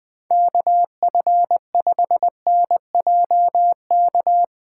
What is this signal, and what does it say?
Morse code, heard as a keyed tone, about 21 wpm, 700 Hz. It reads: KF5NJK
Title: USA Callsign DE Callsign - 6 Characters - 20wpm